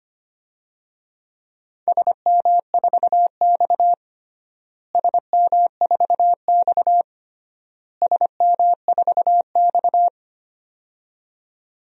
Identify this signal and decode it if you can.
Morse code, heard as a keyed tone, about 25 wpm, 700 Hz. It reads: SM4X SM4X SM4X
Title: DXCC-100 Call Signs - 4 Characters - Repeated 3x - 25wpm